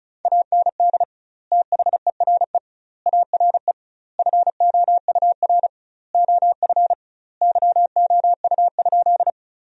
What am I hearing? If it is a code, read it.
Morse code, heard as a keyed tone, about 34 wpm, 700 Hz. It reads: AND THERE ARE FOUR OF YOU?
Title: 2x Word-Spacing - Sentences from Top 300 Words - 35wpm